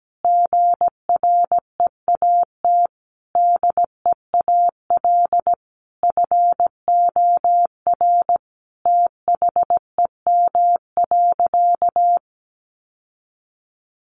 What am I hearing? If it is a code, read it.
Morse code, heard as a keyed tone, about 17 wpm, 700 Hz. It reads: GREAT DEAL FOR THEM.
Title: Sentences from Top 800 Words 17wpm